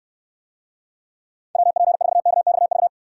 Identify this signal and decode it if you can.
Morse code, heard as a keyed tone, about 69 wpm, 700 Hz. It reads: WP4KXV